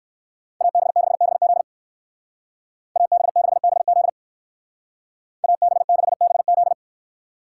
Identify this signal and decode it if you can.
Morse code, heard as a keyed tone, about 54 wpm, 700 Hz. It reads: AB6BZ AB6BZ AB6BZ
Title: Speed Racing - USA 2x2 Call Signs 45wpm